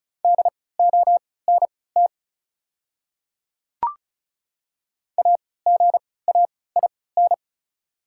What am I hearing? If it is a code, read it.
Morse code, heard as a keyed tone, about 34 wpm, spaced as for 20 wpm, 700 Hz. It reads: DONT AGAIN
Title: 3x Character Spacing - Top 300 Words - 35wpm